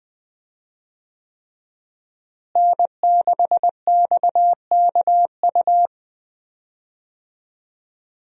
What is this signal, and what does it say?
Morse code, heard as a keyed tone, about 20 wpm, 700 Hz. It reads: N6XKU
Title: USA Call Signs - 5 Characters - Repeated 3x - 20wpm